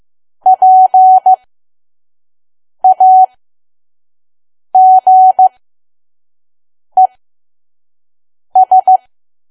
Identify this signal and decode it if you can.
Morse code, heard as a keyed tone, about 15 wpm, spaced as for 5 wpm, 750 Hz. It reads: PAGES
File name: Archive / 5 WPM files / PM